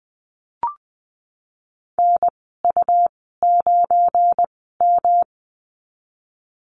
Morse code, heard as a keyed tone, about 20 wpm, spaced as for 15 wpm, 700 Hz. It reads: NU9M
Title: 179 Lesson 29g - Slash - Call Signs 20wpm